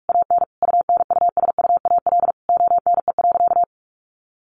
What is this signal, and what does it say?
Morse code, heard as a keyed tone, about 44 wpm, 700 Hz. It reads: AN UNUSUAL ONE.
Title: Sentences from Top 1400 Words 45wpm